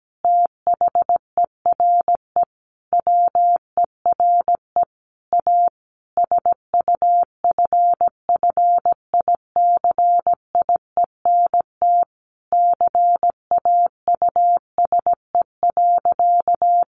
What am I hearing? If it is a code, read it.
Morse code, heard as a keyed tone, about 17 wpm, 700 Hz. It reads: THERE WERE A SUFFICIENT CAUSE.
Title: Sentences from Top 1400 Words 17wpm